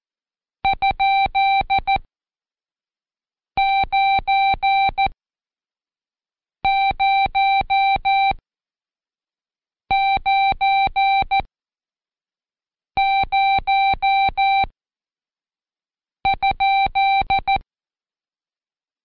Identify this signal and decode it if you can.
Morse code, heard as a keyed tone, about 14 wpm, spaced as for 5 wpm, 780 Hz. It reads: ?9090?